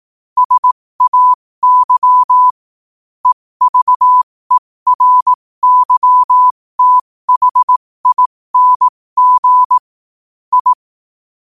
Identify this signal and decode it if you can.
Morse code, heard as a keyed tone, about 18 wpm, spaced as for 15 wpm, 1000 Hz. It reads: SAY EVERYTHING I
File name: qotdfiles / 15wpm